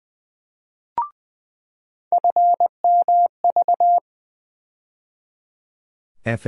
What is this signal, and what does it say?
Morse code, heard as a keyed tone, about 20 wpm, 700 Hz. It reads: FMV
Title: USA Call Sign Postfixes - 3 Letters 20wpm